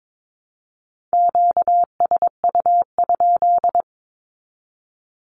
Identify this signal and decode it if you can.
Morse code, heard as a keyed tone, about 22 wpm, 700 Hz. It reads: QSU?
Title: Q-Codes with Spoken Meaning - 22wpm